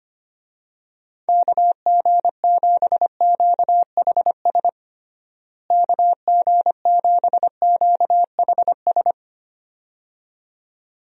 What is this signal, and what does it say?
Morse code, heard as a keyed tone, about 25 wpm, 700 Hz. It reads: KG7QHS KG7QHS
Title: USA Call Signs - 6 Characters - Repeated 2x - 25wpm